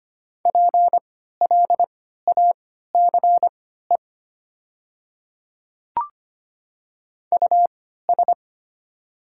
Morse code, heard as a keyed tone, about 25 wpm, spaced as for 14 wpm, 700 Hz. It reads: PLACE US
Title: 3x Character Spacing - Top 200 Words - 25wpm